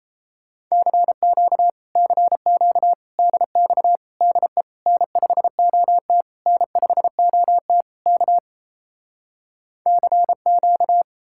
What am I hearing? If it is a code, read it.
Morse code, heard as a keyed tone, about 38 wpm, 700 Hz. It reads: CQ CQ DX DE N5OT N5OT K CQ